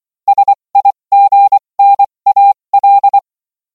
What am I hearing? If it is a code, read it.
Morse code, heard as a keyed tone, about 24 wpm, spaced as for 20 wpm, 800 Hz. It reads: SIGNAL